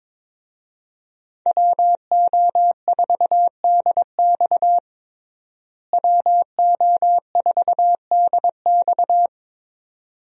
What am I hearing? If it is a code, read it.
Morse code, heard as a keyed tone, about 22 wpm, 700 Hz. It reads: WO4DX WO4DX
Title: USA Call Signs - 5 Characters - Repeated 2x - 22wpm